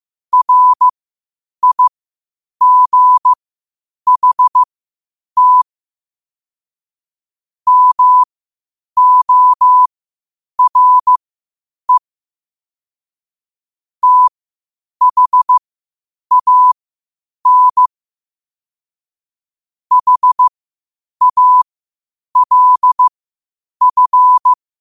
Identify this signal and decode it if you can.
Morse code, heard as a keyed tone, about 15 wpm, spaced as for 8 wpm, 1000 Hz. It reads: RIGHT MORE THAN HALF
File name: qotdfiles / 8wpm